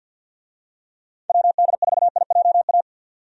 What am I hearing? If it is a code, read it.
Morse code, heard as a keyed tone, about 51 wpm, 700 Hz. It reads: WD4IJA
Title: USA Call Signs - 6 Characters - 50wpm